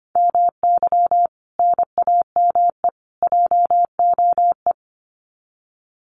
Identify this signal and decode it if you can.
Morse code, heard as a keyed tone, about 25 wpm, 700 Hz. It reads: MY NAME JOE
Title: Common QSO Phrases 25wpm